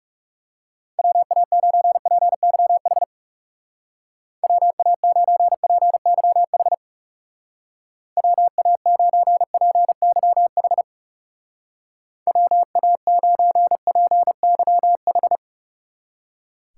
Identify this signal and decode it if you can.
Morse code, heard as a keyed tone, about 36 wpm, 700 Hz. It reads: WA9PYH WA9PYH WA9PYH WA9PYH
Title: Speed Racing - USA 2x3 Call Signs 30wpm